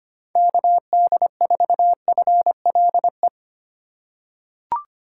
Morse code, heard as a keyed tone, about 25 wpm, 700 Hz. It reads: KD4FLE
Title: USA Call Signs - 6 Characters - 4x Character Spacing - 25wpm